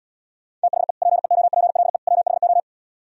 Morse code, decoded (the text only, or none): THE PEOPLE WHO